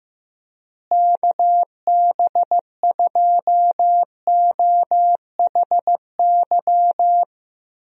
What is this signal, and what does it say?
Morse code, heard as a keyed tone, about 15 wpm, 700 Hz. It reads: KB2OHY
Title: USA Call Signs - 6 Characters - Repeated 3x - 15wpm